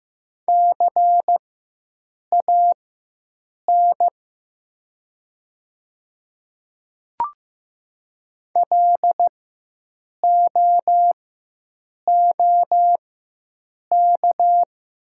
Morse code, heard as a keyed tone, about 15 wpm, spaced as for 7 wpm, 700 Hz. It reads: CAN LOOK
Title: 4x Character Spacing - Top 100 Words - 15wpm